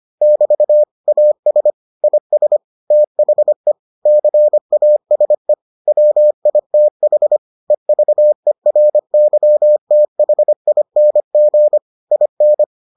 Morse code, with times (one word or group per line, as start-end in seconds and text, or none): 0.2-0.8 <BT>
1.1-1.7 AS
2.0-2.6 IS
2.9-3.7 THE
4.1-5.5 CASE
5.9-7.4 WITH
7.7-11.8 EVERYTHING
12.1-12.6 IN